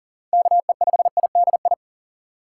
Morse code, heard as a keyed tone, about 39 wpm, 700 Hz. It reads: KE5IDI
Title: USA Call Signs - 6 Characters - 6x Character Spacing - 40wpm